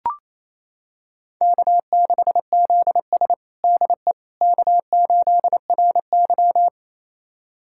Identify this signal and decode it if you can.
Morse code, heard as a keyed tone, about 28 wpm, 700 Hz. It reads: K6ZS DE K8RY
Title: USA Callsign DE Callsign - 4 Characters - 28wpm